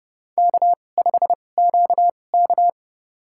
Morse code, heard as a keyed tone, about 30 wpm, spaced as for 22 wpm, 700 Hz. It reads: K5QK
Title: USA Call Signs - 4 Characters - 2x Character Spacing - 30wpm